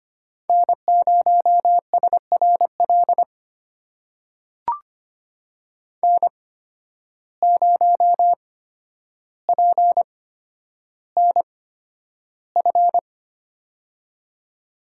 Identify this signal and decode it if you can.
Morse code, heard as a keyed tone, about 25 wpm, spaced as for 11 wpm, 700 Hz. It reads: N0SRL N0PNF
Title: USA Call Signs - 5 Characters - 8x Character Spacing - 25wpm